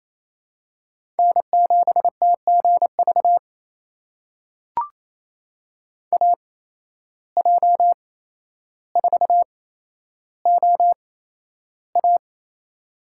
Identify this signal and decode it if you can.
Morse code, heard as a keyed tone, about 28 wpm, spaced as for 12 wpm, 700 Hz. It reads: N7TGV AJ4OA
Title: USA Call Signs - 5 Characters - 8x Character Spacing - 28wpm